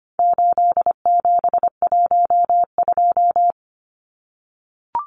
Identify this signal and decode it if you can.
Morse code, heard as a keyed tone, about 25 wpm, 700 Hz. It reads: 8712
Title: Numbers - 4-Digit 25wpm